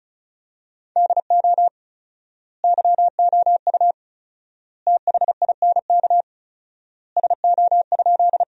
Morse code, heard as a keyed tone, about 34 wpm, 700 Hz. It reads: DO YOU THINK SO?
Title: 4x Word-Spacing - Sentences from Top 200 Words - 35wpm